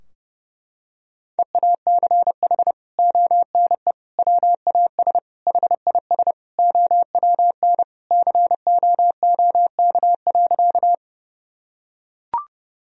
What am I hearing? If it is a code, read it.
Morse code, heard as a keyed tone, about 32 wpm, 700 Hz. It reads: EACH ONE WAS HIS OWN COOK.
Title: Sentences from Top 1000 Words - Review Part 4 - 30wpm